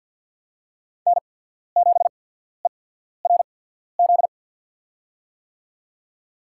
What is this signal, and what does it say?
Morse code, heard as a keyed tone, about 53 wpm, spaced as for 14 wpm, 700 Hz. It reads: N7ERZ
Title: USA Call Signs - 5 Characters - 8x Character Spacing - 50wpm